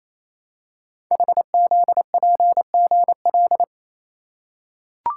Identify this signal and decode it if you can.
Morse code, heard as a keyed tone, about 27 wpm, 700 Hz. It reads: HZPGL
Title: Code Groups - 5 Letters - 28wpm